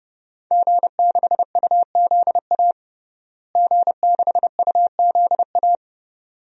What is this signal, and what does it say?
Morse code, heard as a keyed tone, about 30 wpm, 700 Hz. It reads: G6UZA G6UZA
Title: DXCC-100 Call Signs - 5 Characters - Repeated 2x - 30wpm